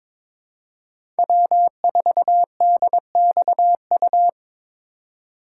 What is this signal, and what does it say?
Morse code, heard as a keyed tone, about 22 wpm, 700 Hz. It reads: W4DXU
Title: USA Call Signs - 5 Characters - 4x Character Spacing - 22wpm